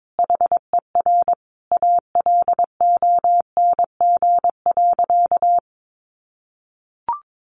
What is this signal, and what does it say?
Morse code, heard as a keyed tone, about 22 wpm, 700 Hz. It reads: HER ALONG.
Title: Sentences from Top 1500 Words 22wpm